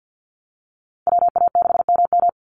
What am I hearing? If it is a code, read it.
Morse code, heard as a keyed tone, about 50 wpm, 700 Hz. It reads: WA6MM
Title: NA SOTA Activator Callsigns 50wpm